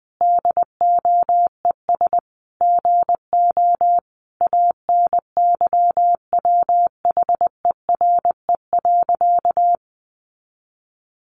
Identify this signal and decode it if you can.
Morse code, heard as a keyed tone, about 20 wpm, 700 Hz. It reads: DOES GO ANYWHERE.
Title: Sentences from Top 1400 Words 20wpm